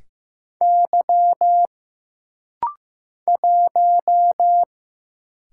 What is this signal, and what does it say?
Morse code, heard as a keyed tone, about 15 wpm, 700 Hz. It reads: Y 1
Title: Single Letter-Number Alt - Rapid-Fire 15wpm